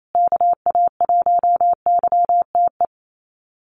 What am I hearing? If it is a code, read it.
Morse code, heard as a keyed tone, about 28 wpm, 700 Hz. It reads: KA1YTE
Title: USA Call Signs Repeated 3x 28wpm